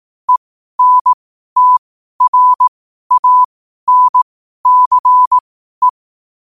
Morse code, heard as a keyed tone, about 18 wpm, spaced as for 12 wpm, 1000 Hz. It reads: ENTRANCE
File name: qotdfiles / 12wpm